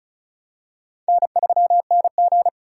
Morse code, heard as a keyed tone, about 36 wpm, 700 Hz. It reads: N3NG
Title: USA Call Signs - 4 Characters - 2x Character Spacing - 35wpm